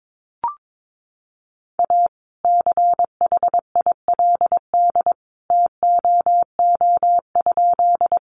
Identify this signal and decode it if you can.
Morse code, heard as a keyed tone, about 22 wpm, 700 Hz. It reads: A CHILD TOO?
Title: Sentences from Top 800 Words 22wpm